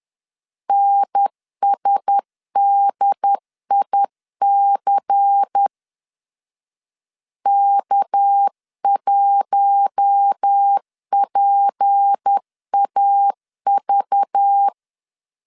Morse code, heard as a keyed tone, about 10 wpm, 800 Hz. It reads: NSDIC K1PAV
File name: CW / DET / 1 6 1500C S C 10wpm 4252